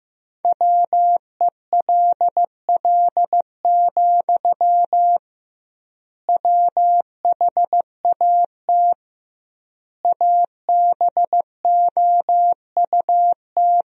Morse code, heard as a keyed tone, about 15 wpm, 700 Hz. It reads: WELL, WHAT ABOUT
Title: 2x Word-Spacing - Sentences from Top 500 Words - 15wpm